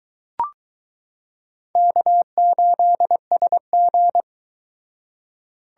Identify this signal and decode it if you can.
Morse code, heard as a keyed tone, about 22 wpm, 700 Hz. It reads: K8SG